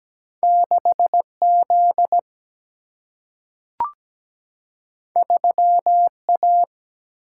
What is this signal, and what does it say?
Morse code, heard as a keyed tone, about 17 wpm, 700 Hz. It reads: 6Z 3A
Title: Single-Letter Single-Number 17wpm